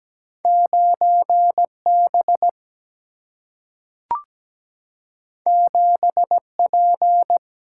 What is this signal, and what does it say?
Morse code, heard as a keyed tone, about 17 wpm, 700 Hz. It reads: 9B 7P
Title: Single-Letter Single-Number 17wpm